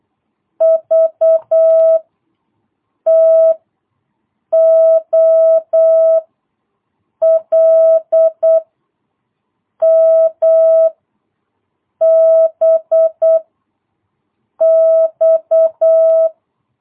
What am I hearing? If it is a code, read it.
Morse code, heard as a keyed tone, about 8 wpm, 660 Hz. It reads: V T O L M B X
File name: jh8cbh / 2022cw